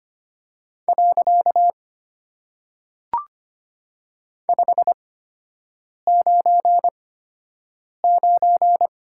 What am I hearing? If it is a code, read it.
Morse code, heard as a keyed tone, about 25 wpm, 700 Hz. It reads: . 5 9 9
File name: Top 90 QSO Elements - 8x Character Spacing - 25wpm